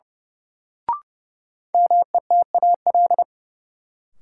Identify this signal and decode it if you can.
Morse code, heard as a keyed tone, about 32 wpm, 700 Hz. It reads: METAL
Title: Sentences from Top 1000 Words - Review Part 6 - New Word Drill - 30wpm